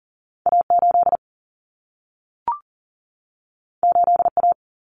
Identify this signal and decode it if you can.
Morse code, heard as a keyed tone, about 40 wpm, 700 Hz. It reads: A8 8A
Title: Common Cut Numbers - 2-Digit 40wpm